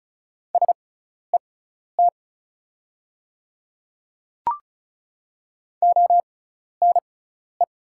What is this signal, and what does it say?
Morse code, heard as a keyed tone, about 34 wpm, spaced as for 12 wpm, 690 Hz. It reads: SET ONE
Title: 6x Character Spacing - Top 300 Words - 35wpm